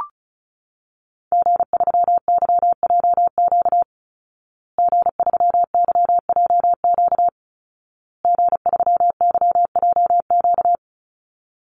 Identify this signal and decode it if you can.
Morse code, heard as a keyed tone, about 34 wpm, 700 Hz. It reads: G3YJQ G3YJQ G3YJQ
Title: Call Signs Repeated 2x - 35wpm